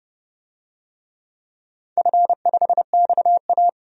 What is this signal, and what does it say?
Morse code, heard as a keyed tone, about 30 wpm, 700 Hz. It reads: F5XA
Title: DXCC-100 Call Signs - 4 Characters - 30wpm